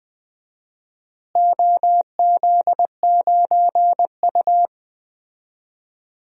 Morse code, heard as a keyed tone, about 20 wpm, 700 Hz. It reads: OZ9U
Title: DXCC-100 Call Signs - 4 Characters - Repeated 2x - 20wpm